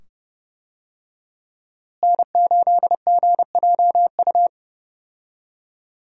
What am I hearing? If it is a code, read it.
Morse code, heard as a keyed tone, about 30 wpm, 700 Hz. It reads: N8GJU
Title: USA Call Signs - 5 Characters - Repeated 2x - 30wpm